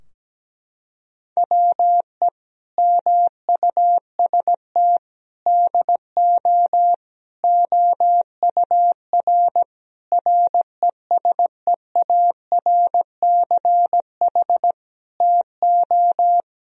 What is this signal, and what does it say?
Morse code, heard as a keyed tone, about 17 wpm, 700 Hz. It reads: WE MUST DO OUR RESEARCH TO
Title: Sentences from Top 1000 Words - Review Part 4 - 17wpm